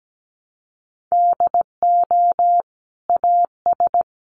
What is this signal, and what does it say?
Morse code, heard as a keyed tone, about 17 wpm, 700 Hz. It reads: DO AS
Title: Sentences from Top 100 Words 17wpm